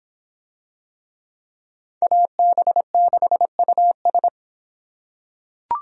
Morse code, heard as a keyed tone, about 24 wpm, 700 Hz. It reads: AB6US